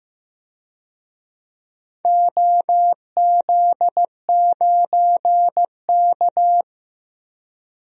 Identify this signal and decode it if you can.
Morse code, heard as a keyed tone, about 15 wpm, 700 Hz. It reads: OZ9K
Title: DXCC-100 Call Signs - 4 Characters - Repeated 3x - 15wpm